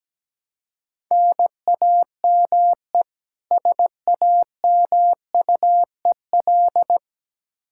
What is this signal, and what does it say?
Morse code, heard as a keyed tone, about 17 wpm, 700 Hz. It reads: NAME SAMUEL
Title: Practice QSO - Exchange 1 - Beginning - 17wpm